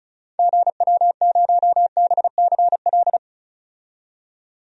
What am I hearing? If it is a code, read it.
Morse code, heard as a keyed tone, about 36 wpm, 700 Hz. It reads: GW0BCL
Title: DXCC-100 Call Signs - 6 Characters - 35wpm